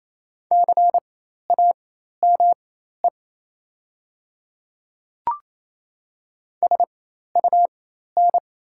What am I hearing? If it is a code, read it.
Morse code, heard as a keyed tone, about 28 wpm, spaced as for 13 wpm, 690 Hz. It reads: CAME SUN
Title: 4x Character Spacing - Top 300 Words - 28wpm